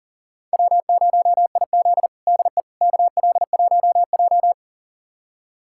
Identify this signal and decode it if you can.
Morse code, heard as a keyed tone, about 40 wpm, 700 Hz. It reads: W0IZ DE KL1J
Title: USA Callsign DE Callsign - 4 Characters - 40wpm